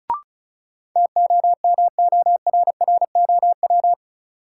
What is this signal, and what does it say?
Morse code, heard as a keyed tone, about 34 wpm, 700 Hz. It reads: TOMORROW
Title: Sentences from Top 1100 Words - New Word Drill - 35wpm